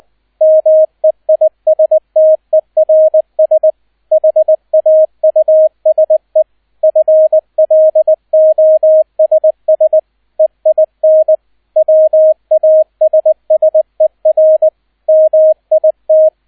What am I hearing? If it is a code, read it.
Morse code, heard as a keyed tone, about 20 wpm, 630 Hz, with radio noise. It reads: MEISTERS HAUSE FLOSS EIN WASSER MIT